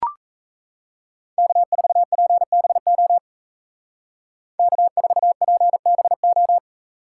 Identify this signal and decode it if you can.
Morse code, heard as a keyed tone, about 33 wpm, 700 Hz. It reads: K4PBO K4PBO
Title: Speed Racing - USA 1x3 Call Signs 28wpm